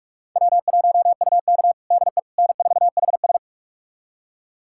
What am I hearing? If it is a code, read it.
Morse code, heard as a keyed tone, about 46 wpm, 700 Hz. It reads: W1UK DE N4HS
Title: USA Callsign DE Callsign - 4 Characters - 45wpm